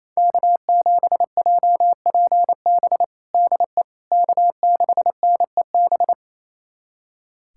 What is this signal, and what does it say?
Morse code, heard as a keyed tone, about 28 wpm, 700 Hz. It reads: K7JPB DE K6NEB
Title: USA Callsign DE Callsign 28wpm